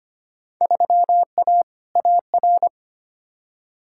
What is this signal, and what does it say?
Morse code, heard as a keyed tone, about 25 wpm, 700 Hz. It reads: 3A AR
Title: Field Day Exchanges - 25wpm